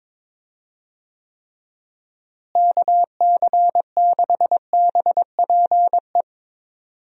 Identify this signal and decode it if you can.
Morse code, heard as a keyed tone, about 21 wpm, 700 Hz. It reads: KC6BPE
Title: USA Call Signs - 6 Characters - Repeated 2x - 22wpm